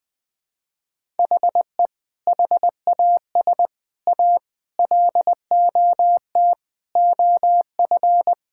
Morse code, heard as a keyed tone, about 20 wpm, 700 Hz. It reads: HE HAS A LOT OF
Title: Sentences from Top 1000 Words - Review Part 6 - 20wpm